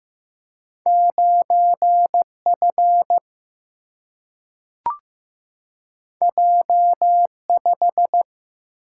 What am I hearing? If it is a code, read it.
Morse code, heard as a keyed tone, about 15 wpm, 700 Hz. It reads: 9 F J 5
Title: Single-Letter Single-Number 15wpm